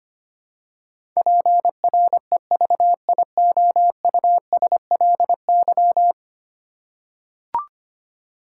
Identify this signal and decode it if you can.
Morse code, heard as a keyed tone, about 25 wpm, 700 Hz. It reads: PREVIOUSLY
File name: 10 Letter Words - First 3 Letters Given - 25wpm